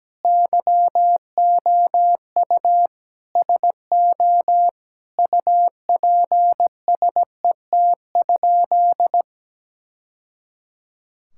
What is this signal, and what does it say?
Morse code, heard as a keyed tone, about 17 wpm, 700 Hz. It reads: YOU SO UPSET?
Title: Sentences from Top 1000 Words - Review Part 5 - 17wpm